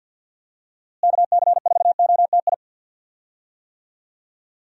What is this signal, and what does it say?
Morse code, heard as a keyed tone, about 49 wpm, 700 Hz. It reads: KC4OTI
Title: USA Call Signs - 6 Characters - Repeated 2x - 50wpm